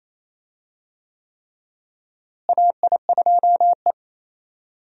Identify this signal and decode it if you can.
Morse code, heard as a keyed tone, about 28 wpm, 700 Hz. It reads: AI2E
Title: USA Call Signs - 4 Characters - Repeated 3x - 28wpm